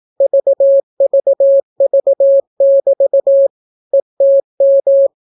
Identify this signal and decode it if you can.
Morse code, heard as a keyed tone, about 18 wpm, 550 Hz. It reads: VVV<BT> ETM